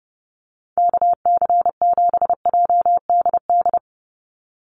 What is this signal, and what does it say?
Morse code, heard as a keyed tone, about 28 wpm, 700 Hz. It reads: KC7JDD
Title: USA Call Signs 30wpm